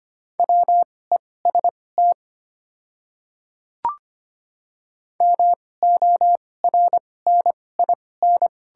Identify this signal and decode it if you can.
Morse code, heard as a keyed tone, about 25 wpm, spaced as for 18 wpm, 690 Hz. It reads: WEST MORNIN
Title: 2x Character Spacing - Top 500 Words - 25wpm